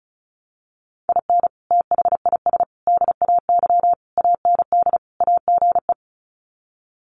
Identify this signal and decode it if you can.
Morse code, heard as a keyed tone, about 34 wpm, 700 Hz. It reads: IN THIS DAY AND AGE
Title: Sets of 5 Words Encore 35wpm